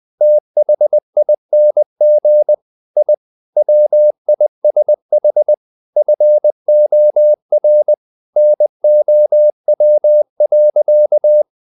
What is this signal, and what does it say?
Morse code, heard as a keyed tone, about 20 wpm, 600 Hz. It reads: THING I WISH FOR NOW.
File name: the old man and the sea part 2 20wpm